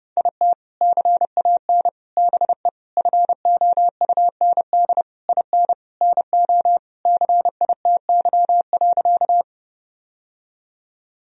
Morse code, heard as a keyed tone, about 30 wpm, 700 Hz. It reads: IT CAN BE FOUND IN NO CITY.
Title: Sentences from Top 1000 Words - Review Part 1 - 30wpm